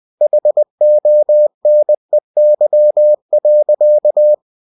Morse code, heard as a keyed tone, about 20 wpm, 600 Hz. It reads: HONEY.
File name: cw-books / winnie-the-pooh / winnie-20wpm-page-0037